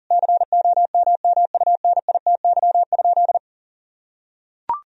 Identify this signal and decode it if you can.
Morse code, heard as a keyed tone, about 40 wpm, 700 Hz. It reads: COMMUNITY?
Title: Sentences from Top 1000 Words - Review Part 4 - 40wpm